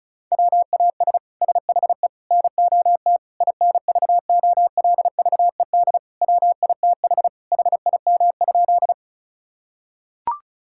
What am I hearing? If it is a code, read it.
Morse code, heard as a keyed tone, about 36 wpm, 700 Hz. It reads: WAS SHE NOT INVOLVED WITH HIM?
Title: Sentences from Top 1000 Words - Review Part 5 - 35wpm